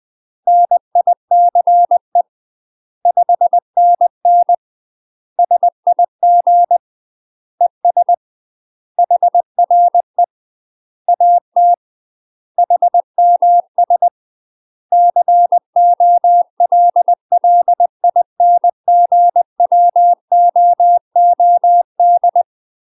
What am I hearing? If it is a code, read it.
Morse code, heard as a keyed tone, about 20 wpm, 700 Hz. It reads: NICE 5NN SIG ES HRE AT HMS COLLINGWOOD
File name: UK MASTER QSOs TRAINING / 20WPM / UK MASTER QSOs 0028